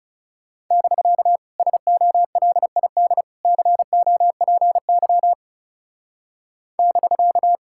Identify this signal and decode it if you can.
Morse code, heard as a keyed tone, about 36 wpm, 700 Hz. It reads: <BK> SOLID COPY <BK>
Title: Speed-Racing Common QSO Phrases Encore - 30wpm